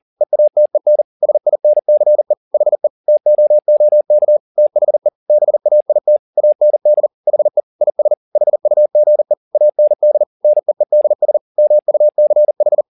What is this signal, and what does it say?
Morse code, heard as a keyed tone, about 40 wpm, 600 Hz. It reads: EATEN SINCE HE TOOK THE BAIT AND HE IS HUGE AND NEEDS MUCH
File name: the old man and the sea 2 40wpm